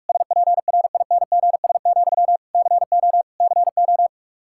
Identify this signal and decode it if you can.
Morse code, heard as a keyed tone, about 46 wpm, 700 Hz. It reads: SPRINGS, CO CO
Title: K1USN SST QSO Practice 45wpm